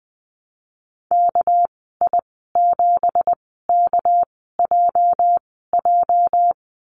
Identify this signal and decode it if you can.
Morse code, heard as a keyed tone, about 20 wpm, spaced as for 15 wpm, 700 Hz. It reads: KI7KJJ
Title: 213 Lesson 34g - Number 8 - Call Signs 20wpm